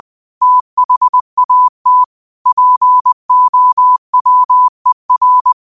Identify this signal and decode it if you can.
Morse code, heard as a keyed tone, about 20 wpm, 1000 Hz. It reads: THAT POWER